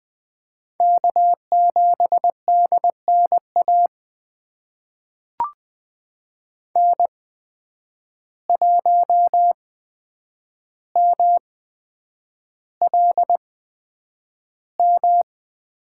Morse code, heard as a keyed tone, about 20 wpm, spaced as for 9 wpm, 700 Hz. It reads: K7DNA N1MLM
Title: USA Call Signs - 5 Characters - 8x Character Spacing - 20wpm